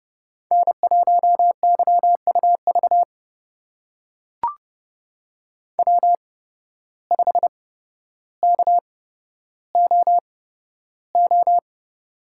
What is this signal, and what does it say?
Morse code, heard as a keyed tone, about 30 wpm, spaced as for 13 wpm, 700 Hz. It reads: N1YUV W5KOO
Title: USA Call Signs - 5 Characters - 8x Character Spacing - 30wpm